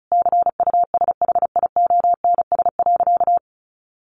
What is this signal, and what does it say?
Morse code, heard as a keyed tone, about 34 wpm, 700 Hz. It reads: CUSHIONS.
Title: A Sound Of Thunder 35wpm